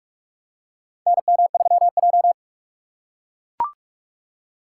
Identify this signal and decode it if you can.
Morse code, heard as a keyed tone, about 42 wpm, 700 Hz. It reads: NM3J
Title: USA Call Signs - 6x Character Spacing - 45wpm